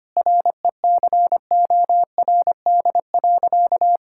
Morse code, heard as a keyed tone, about 25 wpm, 700 Hz. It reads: RECORD.
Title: Sentences from Top 1000 Words - Review Part 6 - 25wpm